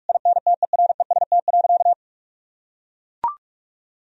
Morse code, heard as a keyed tone, about 41 wpm, 700 Hz. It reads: INTEREST.
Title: Sentences from Top 1000 Words - Review Part 4 - 45wpm